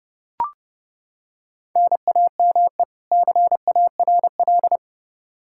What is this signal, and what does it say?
Morse code, heard as a keyed tone, about 32 wpm, 700 Hz. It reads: NAME CARL
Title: Practice QSO - Exchange 1 - Beginning - 30wpm